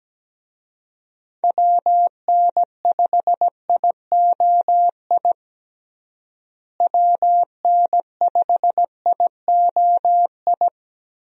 Morse code, heard as a keyed tone, about 17 wpm, 700 Hz. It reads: WN5IOI WN5IOI
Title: USA Call Signs - 6 Characters - Repeated 2x - 17wpm